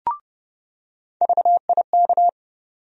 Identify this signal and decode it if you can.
Morse code, heard as a keyed tone, about 30 wpm, 700 Hz. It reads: VIK
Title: USA Call Sign Postfixes - 3 Letters 30wpm